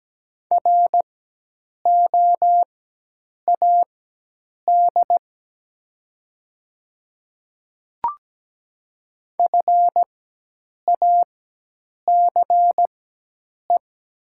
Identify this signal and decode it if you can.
Morse code, heard as a keyed tone, about 17 wpm, spaced as for 8 wpm, 700 Hz. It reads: ROAD FACE
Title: 4x Character Spacing - Top 500 Words - 17wpm